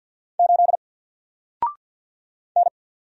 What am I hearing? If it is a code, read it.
Morse code, heard as a keyed tone, about 54 wpm, 700 Hz. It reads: 8 N
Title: Single Character (Letter:Number:Punctuation:Prosign) 50wpm